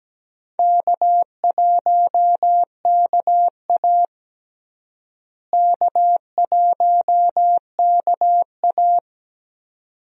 Morse code, heard as a keyed tone, about 17 wpm, 700 Hz. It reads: K1KA K1KA
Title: USA Call Signs - 4 Characters - Repeated 2x - 17wpm